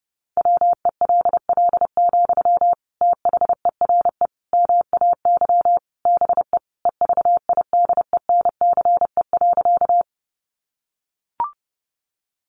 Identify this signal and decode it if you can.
Morse code, heard as a keyed tone, about 30 wpm, 700 Hz. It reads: WELL, THERE MAY BE EVIDENCE.
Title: Sentences from Top 1300 Words 30wpm